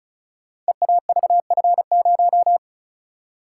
Morse code, heard as a keyed tone, about 36 wpm, 700 Hz. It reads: EAVF0
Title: Code Groups - 5 Letter:Numbers - 35wpm